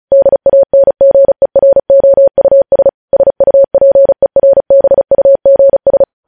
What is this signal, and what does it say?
Morse code, heard as a keyed tone, about 36 wpm, 550 Hz. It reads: DANGEROUS SUPERBUGS
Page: news headlines in morse code at 35 wpm